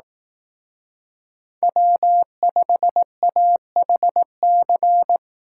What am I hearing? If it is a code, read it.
Morse code, heard as a keyed tone, about 18 wpm, 700 Hz. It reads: W5AHC
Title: Speed Racing - USA 1x3 Call Signs 15wpm